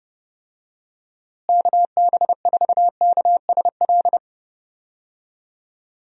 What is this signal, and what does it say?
Morse code, heard as a keyed tone, about 30 wpm, 700 Hz. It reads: KB4KSL
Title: USA Call Signs - 6 Characters - Repeated 3x - 30wpm